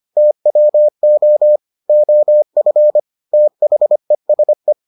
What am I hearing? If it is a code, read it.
Morse code, heard as a keyed tone, about 25 wpm, 600 Hz. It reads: TWO OF THESE